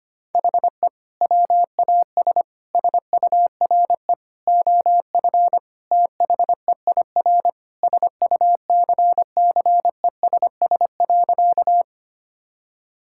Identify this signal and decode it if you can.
Morse code, heard as a keyed tone, about 25 wpm, 700 Hz. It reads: HE WAS SURE OF THEIR SUCCESS.
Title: Sentences from Top 1000 Words - Review Part 6 - 25wpm